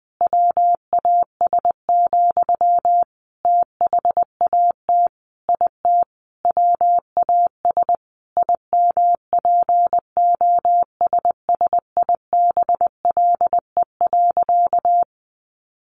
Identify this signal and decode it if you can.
Morse code, heard as a keyed tone, about 20 wpm, 700 Hz. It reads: WAS, THAT IT WAS IMPOSSIBLE.
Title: Sentences from Top 1100 Words 20wpm